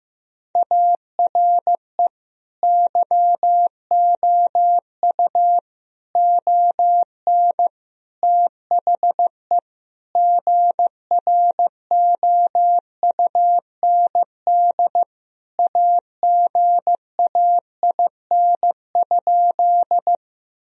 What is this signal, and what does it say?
Morse code, heard as a keyed tone, about 15 wpm, 700 Hz. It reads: ARE YOU ON THE GROUND AGAIN?
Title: Sentences from Top 1000 Words - Review Part 2 - 15wpm